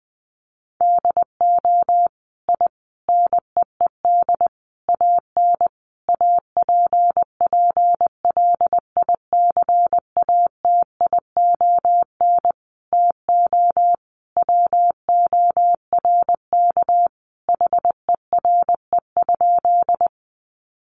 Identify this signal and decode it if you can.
Morse code, heard as a keyed tone, about 20 wpm, 700 Hz. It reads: DO I NEED AN APPLICATION TO WORK HERE?
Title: Sentences from Top 1200 Words 20wpm